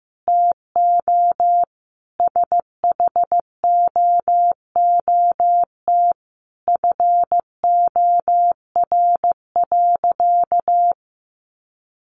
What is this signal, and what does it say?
Morse code, heard as a keyed tone, about 15 wpm, 700 Hz. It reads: TO SHOOT FOR.
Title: Sentences from Top 1400 Words 15wpm